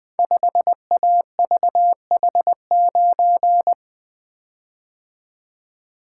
Morse code, heard as a keyed tone, about 20 wpm, 700 Hz. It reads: 5AVH9
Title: Code Groups - 5 Letter:Numbers - 20wpm